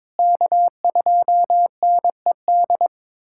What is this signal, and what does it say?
Morse code, heard as a keyed tone, about 22 wpm, 700 Hz. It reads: K2NED
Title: USA Callsign DE Callsign - 5 Characters - 22wpm